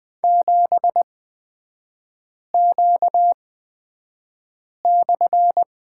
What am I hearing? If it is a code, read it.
Morse code, heard as a keyed tone, about 20 wpm, spaced as for 5 wpm, 700 Hz. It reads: 7Q/
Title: 255 Lesson 41e - Period - Three Characters 20wpm